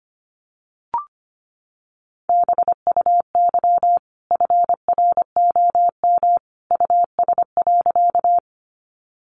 Morse code, heard as a keyed tone, about 25 wpm, 700 Hz. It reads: BUY FROM US.
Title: Sentences from Top 700 Words 25wpm